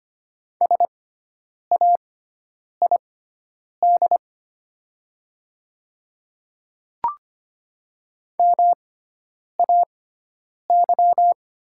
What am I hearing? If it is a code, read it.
Morse code, heard as a keyed tone, about 25 wpm, spaced as for 9 wpm, 700 Hz. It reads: SAID MAY
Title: 6x Character Spacing - Top 100 Words - 25wpm